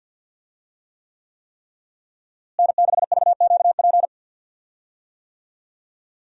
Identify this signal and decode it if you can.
Morse code, heard as a keyed tone, about 52 wpm, 700 Hz. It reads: N6VQP
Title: USA Call Signs - 5 Characters - Repeated 3x - 50wpm